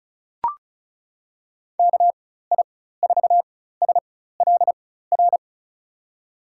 Morse code, heard as a keyed tone, about 36 wpm, spaced as for 16 wpm, 700 Hz. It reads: KI4SLR
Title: USA Call Signs - 6 Characters - 4x Character Spacing - 35wpm